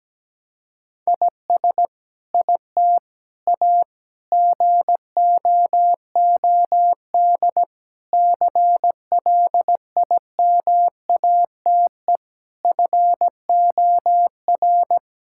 Morse code, heard as a keyed tone, about 17 wpm, 700 Hz. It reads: IS IT A GOOD CLIMATE FOR
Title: Sentences from Top 1600 Words 17wpm